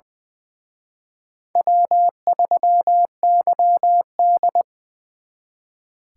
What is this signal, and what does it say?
Morse code, heard as a keyed tone, about 25 wpm, 700 Hz. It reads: W3YD
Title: Speed Racing - USA Call Signs 20wpm